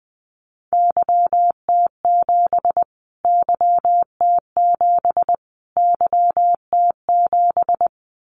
Morse code, heard as a keyed tone, about 20 wpm, 700 Hz. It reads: YT7 YT7 YT7
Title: Callsign Prefixes for Top 25 DXCC - Repeated 3x - 20wpm